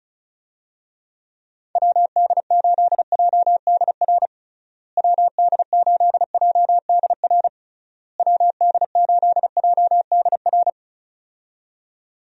Morse code, heard as a keyed tone, about 36 wpm, 700 Hz. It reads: WD8JDR WD8JDR WD8JDR
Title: USA Call Signs - 6 Characters - Repeated 3x - 35wpm